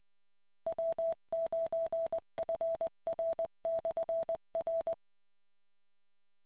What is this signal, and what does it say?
Morse code, heard as a keyed tone, about 20 wpm, 660 Hz. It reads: W9FR/R